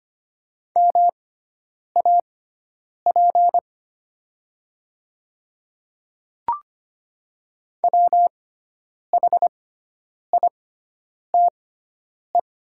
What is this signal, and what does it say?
Morse code, heard as a keyed tone, about 25 wpm, spaced as for 9 wpm, 700 Hz. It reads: MAP WHITE
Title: 6x Character Spacing - Top 500 Words - 25wpm